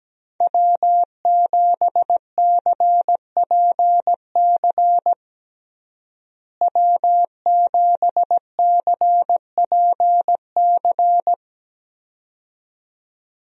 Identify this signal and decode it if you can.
Morse code, heard as a keyed tone, about 17 wpm, 700 Hz. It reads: W7CPC W7CPC
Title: USA Call Signs - 5 Characters - Repeated 2x - 17wpm